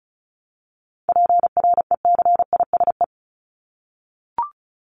Morse code, heard as a keyed tone, about 34 wpm, 700 Hz. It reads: PRECISE
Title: LY Suffix - Top 100 Words with Base Words 35wpm